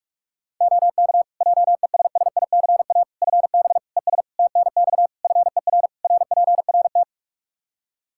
Morse code, heard as a keyed tone, about 47 wpm, 700 Hz. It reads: OK JESSICA FB ES TNX FER RPRT
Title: Practice QSO - Exchange 2 - Info - 45wpm